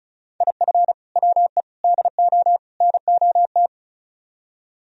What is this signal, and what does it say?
Morse code, heard as a keyed tone, about 44 wpm, 700 Hz. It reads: IF WE DO NOT
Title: Speed-Racing - Sets of 4 Words 35wpm